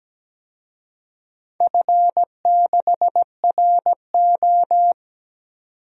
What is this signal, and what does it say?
Morse code, heard as a keyed tone, about 17 wpm, 700 Hz. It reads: F6RO
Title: DXCC-100 Call Signs - 4 Characters - 17wpm